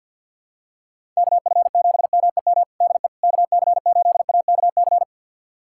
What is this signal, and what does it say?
Morse code, heard as a keyed tone, about 51 wpm, 700 Hz. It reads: KF7MEM DE KC8AKC
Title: USA Callsign DE Callsign - 6 Characters - 50wpm